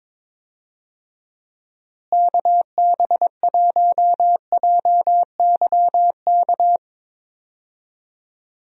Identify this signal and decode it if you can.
Morse code, heard as a keyed tone, about 22 wpm, 700 Hz. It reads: KB1JYK
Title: USA Call Signs - 6 Characters - Repeated 2x - 22wpm